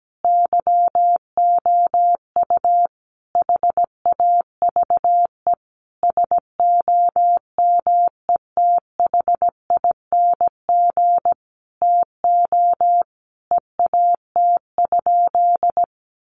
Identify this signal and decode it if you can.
Morse code, heard as a keyed tone, about 17 wpm, 700 Hz. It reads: YOU HAVE SOMETHING TO EAT?
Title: Sentences from Top 600 Words 17wpm